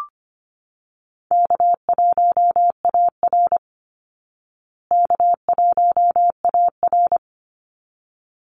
Morse code, heard as a keyed tone, about 25 wpm, 700 Hz. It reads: K1AR K1AR
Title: Easy Call Signs Repeated 2x - 25wpm